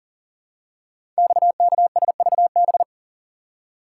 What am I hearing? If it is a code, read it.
Morse code, heard as a keyed tone, about 39 wpm, 700 Hz. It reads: XKSVB
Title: Code Groups - 5 Letters - 40wpm